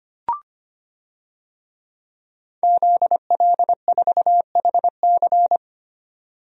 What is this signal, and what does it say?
Morse code, heard as a keyed tone, about 25 wpm, 700 Hz. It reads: ZL4HC
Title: DXCC-100 Call Signs - 5 Characters - 25wpm